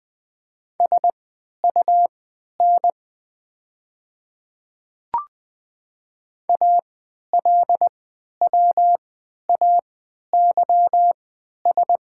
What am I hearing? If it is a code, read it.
Morse code, heard as a keyed tone, about 20 wpm, spaced as for 11 wpm, 690 Hz. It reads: SUN ALWAYS
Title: 3x Character Spacing - Top 300 Words - 20wpm